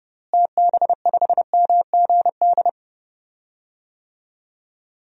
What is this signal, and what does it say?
Morse code, heard as a keyed tone, about 30 wpm, 700 Hz. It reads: TB5MGD
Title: DXCC-100 Call Signs - 6 Characters - Repeated 2x - 30wpm